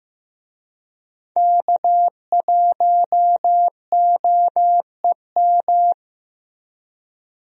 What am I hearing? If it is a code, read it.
Morse code, heard as a keyed tone, about 15 wpm, 700 Hz. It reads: K1OEM
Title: USA Call Signs - 5 Characters - 15wpm